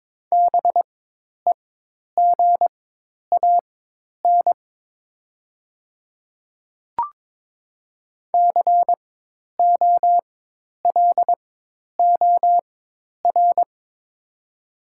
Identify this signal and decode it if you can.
Morse code, heard as a keyed tone, about 22 wpm, spaced as for 10 wpm, 700 Hz. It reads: BEGAN COLOR
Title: 4x Character Spacing - Top 300 Words - 22wpm